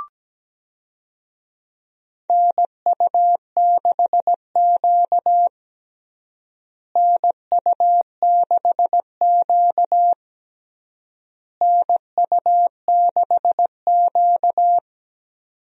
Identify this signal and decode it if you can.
Morse code, heard as a keyed tone, about 17 wpm, 700 Hz. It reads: NU6Q NU6Q NU6Q
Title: USA Call Signs - 4 Characters - Repeated 3x - 17wpm